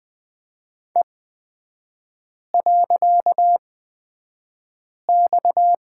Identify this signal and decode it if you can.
Morse code, heard as a keyed tone, about 20 wpm, spaced as for 5 wpm, 700 Hz. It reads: E.X
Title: 255 Lesson 41e - Period - Three Characters 20wpm